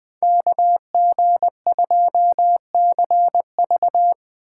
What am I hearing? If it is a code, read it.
Morse code, heard as a keyed tone, about 20 wpm, 700 Hz. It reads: KG2CV
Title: 2023 POTA Hunter Call Signs - 20wpm